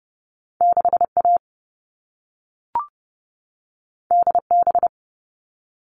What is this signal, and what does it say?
Morse code, heard as a keyed tone, about 30 wpm, 700 Hz. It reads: 6A DB
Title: All Cut Numbers - 2-Digit 30wpm